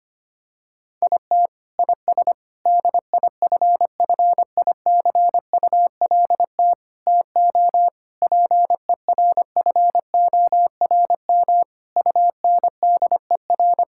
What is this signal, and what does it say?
Morse code, heard as a keyed tone, about 25 wpm, 700 Hz. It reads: IT IS DIFFICULT TO PERFORM UNDER
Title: Sentences from Top 1000 Words - Review Part 6 - 25wpm